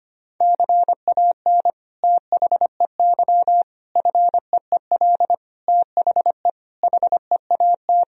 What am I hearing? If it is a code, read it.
Morse code, heard as a keyed tone, about 25 wpm, 700 Hz. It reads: CAN THEY FEEL THE HEAT
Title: Sentences from Top 1000 Words - Review Part 3 - 25wpm